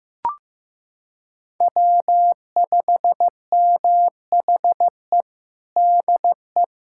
Morse code, heard as a keyed tone, about 15 wpm, 700 Hz. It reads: W5MHE DE
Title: USA Callsign DE Callsign - 5 Characters - 15wpm